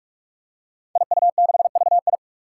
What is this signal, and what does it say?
Morse code, heard as a keyed tone, about 41 wpm, 700 Hz. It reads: IU6VI
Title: Five Challenging Letter-Number Combinations 45wpm